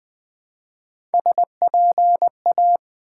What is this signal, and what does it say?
Morse code, heard as a keyed tone, about 20 wpm, 700 Hz. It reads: SPA
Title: Build A Word - 1 Letter At A Time - 5-Letter Words - 20wpm